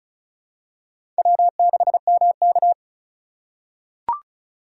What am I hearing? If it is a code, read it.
Morse code, heard as a keyed tone, about 34 wpm, 700 Hz. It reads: W6MK
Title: USA Call Signs - 4 Characters - 3x Character Spacing - 35wpm